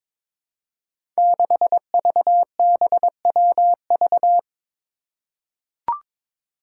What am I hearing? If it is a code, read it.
Morse code, heard as a keyed tone, about 22 wpm, 700 Hz. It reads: 6VBWV
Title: Five Challenging Letter-Number Combinations 22wpm